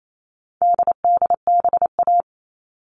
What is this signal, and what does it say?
Morse code, heard as a keyed tone, about 28 wpm, 700 Hz. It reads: DDBA
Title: All Cut Numbers - 4-Digit 28wpm